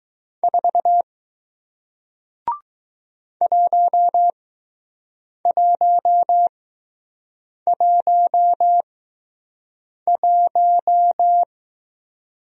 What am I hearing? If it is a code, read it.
Morse code, heard as a keyed tone, about 20 wpm, spaced as for 6 wpm, 700 Hz. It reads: 4 1111